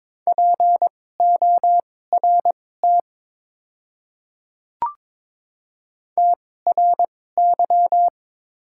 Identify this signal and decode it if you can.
Morse code, heard as a keyed tone, about 22 wpm, spaced as for 16 wpm, 700 Hz. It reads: PORT TRY
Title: 2x Character Spacing - Top 300 Words - 22wpm